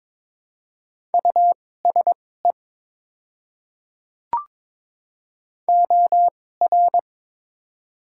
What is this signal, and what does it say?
Morse code, heard as a keyed tone, about 22 wpm, spaced as for 16 wpm, 700 Hz. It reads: USE OR
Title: 2x Character Spacing - Top 100 Words - 22wpm